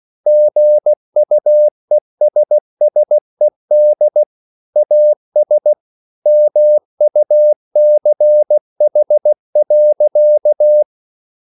Morse code, heard as a keyed tone, about 16 wpm, 600 Hz. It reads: GUESSED AS MUCH.